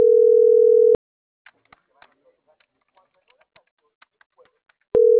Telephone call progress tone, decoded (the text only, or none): ringback tone